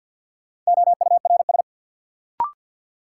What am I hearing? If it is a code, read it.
Morse code, heard as a keyed tone, about 52 wpm, 700 Hz. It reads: OURS
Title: Sentences from Top 1300 Words - New Word Drill Alt - 50wpm